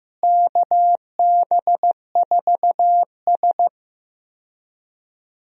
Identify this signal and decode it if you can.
Morse code, heard as a keyed tone, about 15 wpm, 700 Hz. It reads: KB4S
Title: USA Call Signs - 4 Characters - Repeated 2x - 15wpm